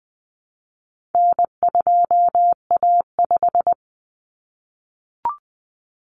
Morse code, heard as a keyed tone, about 20 wpm, 700 Hz. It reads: N2A5
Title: Common Cut Numbers - 4-Digit 20wpm